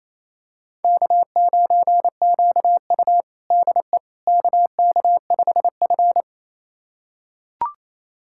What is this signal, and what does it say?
Morse code, heard as a keyed tone, about 28 wpm, 700 Hz. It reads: K9QU DE KK5F
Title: USA Callsign DE Callsign - 4 Characters - 28wpm